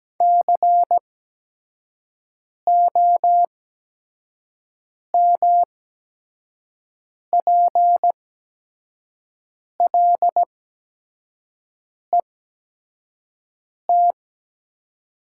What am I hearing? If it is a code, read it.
Morse code, heard as a keyed tone, about 17 wpm, spaced as for 5 wpm, 700 Hz. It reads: COMPLET
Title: 8x Character Spacing - Top 500 Words - 17wpm